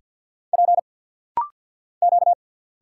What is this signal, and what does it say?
Morse code, heard as a keyed tone, about 50 wpm, 700 Hz. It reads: P Q
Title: Single Letters - Rapid-Fire 50wpm